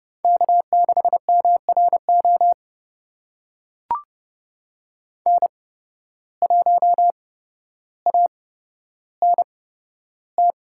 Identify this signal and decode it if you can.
Morse code, heard as a keyed tone, about 30 wpm, spaced as for 13 wpm, 700 Hz. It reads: K6MRO N1ANT
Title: USA Call Signs - 5 Characters - 8x Character Spacing - 30wpm